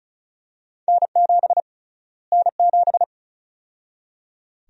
Morse code, heard as a keyed tone, about 36 wpm, 700 Hz. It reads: N7 N7
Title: USA Call Sign Prefixes - Repeated 2x - 35wpm